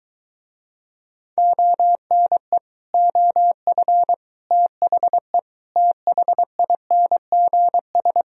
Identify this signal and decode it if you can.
Morse code, heard as a keyed tone, about 23 wpm, 700 Hz. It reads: ONE OF THE THINGS